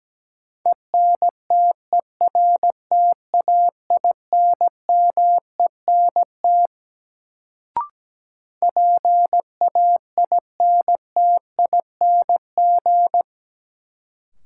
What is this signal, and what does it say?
Morse code, heard as a keyed tone, about 17 wpm, 700 Hz. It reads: ENTERTAINMENT PAINTING
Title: Sentences from Top 1000 Words - Review Part 6 - New Word Drill - 17wpm